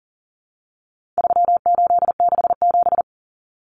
Morse code, heard as a keyed tone, about 40 wpm, 700 Hz. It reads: 3867
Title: Numbers - 4-Digit 40wpm